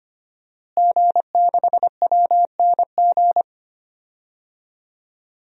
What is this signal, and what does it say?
Morse code, heard as a keyed tone, about 25 wpm, 700 Hz. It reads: G6WNG
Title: DXCC-100 Call Signs - 5 Characters - Repeated 3x - 25wpm